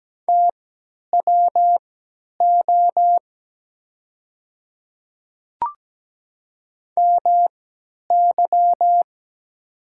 Morse code, heard as a keyed tone, about 17 wpm, spaced as for 10 wpm, 700 Hz. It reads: TWO MY
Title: 3x Character Spacing - Top 100 Words - 17wpm